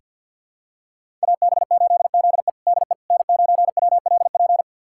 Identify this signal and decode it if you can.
Morse code, heard as a keyed tone, about 48 wpm, 700 Hz. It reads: AB8ZE DE N9WLP
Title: USA Callsign DE Callsign - 5 Characters - 50wpm